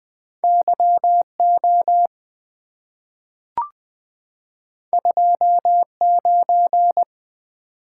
Morse code, heard as a keyed tone, about 20 wpm, 700 Hz. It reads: Y O 2 9
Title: Two Letter-Number Combinations - 20wpm